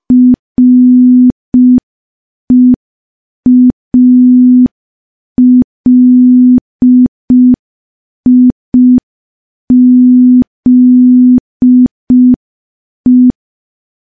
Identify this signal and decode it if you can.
Morse code, heard as a keyed tone, about 5 wpm, 250 Hz. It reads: REALIZE
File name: morse / morse-training-05WPM